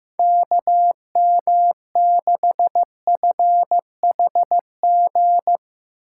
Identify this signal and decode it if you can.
Morse code, heard as a keyed tone, about 16 wpm, 700 Hz. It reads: KM6FHG
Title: Speed Racing - USA 2x3 Call Signs 15wpm